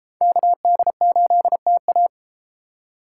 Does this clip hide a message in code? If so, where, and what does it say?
Yes: Morse code from 0.2 to 2.1 s, heard as a keyed tone, about 31 wpm, 700 Hz. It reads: KD8TA